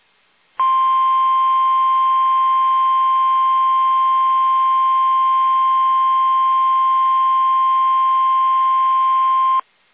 Emergency Alert System (SAME) tones (the 1050 Hz attention tone). It is heard from 0.5 to 9.7 s.
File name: eas / KUSO / KUSO